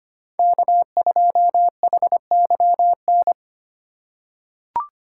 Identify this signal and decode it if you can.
Morse code, heard as a keyed tone, about 25 wpm, 700 Hz. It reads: K2HYN
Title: USA Call Signs - 2x Character Spacing - 25wpm